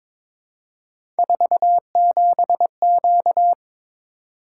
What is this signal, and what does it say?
Morse code, heard as a keyed tone, about 22 wpm, 700 Hz. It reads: 47Q
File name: practice / 22wpm / Three Letter-Number Combinations - 22wpm